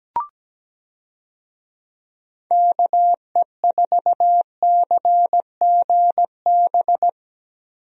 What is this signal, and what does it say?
Morse code, heard as a keyed tone, about 17 wpm, 700 Hz. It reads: KE4CGB
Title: USA Call Signs - 6 Characters - Repeated 3x - 17wpm